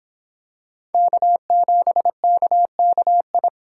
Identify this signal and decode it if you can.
Morse code, heard as a keyed tone, about 25 wpm, 700 Hz. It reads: K7KKI